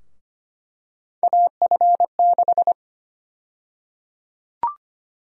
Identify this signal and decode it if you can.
Morse code, heard as a keyed tone, about 25 wpm, 700 Hz. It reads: AF6
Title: USA Call Sign Prefixes - 25wpm